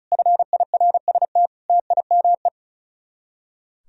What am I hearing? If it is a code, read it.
Morse code, heard as a keyed tone, about 45 wpm, 700 Hz. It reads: FIRST TIME